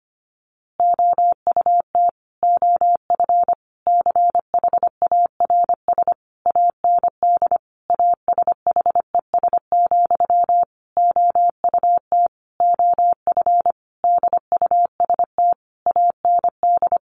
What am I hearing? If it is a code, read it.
Morse code, heard as a keyed tone, about 25 wpm, 700 Hz. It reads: OUT OF CHARS AND ASHES, OUT OF DUST AND
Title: A Sound Of Thunder 25wpm